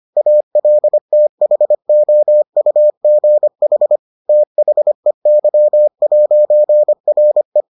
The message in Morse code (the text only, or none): ALTHOUGH THEY'RE